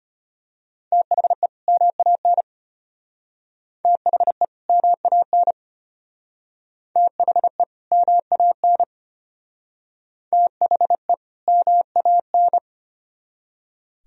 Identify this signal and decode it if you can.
Morse code, heard as a keyed tone, about 30 wpm, 700 Hz. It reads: THE MAN THE MAN THE MAN THE MAN
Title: Speed-Racing - Sets of 2 Words 25wpm